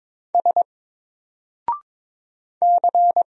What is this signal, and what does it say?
Morse code, heard as a keyed tone, about 22 wpm, 700 Hz. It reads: S C
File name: Single-Letter-Number-22wpm